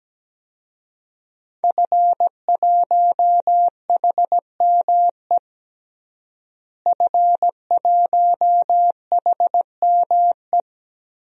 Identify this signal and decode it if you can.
Morse code, heard as a keyed tone, about 17 wpm, 700 Hz. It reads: F1HME F1HME
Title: DXCC-100 Call Signs - 5 Characters - Repeated 3x - 17wpm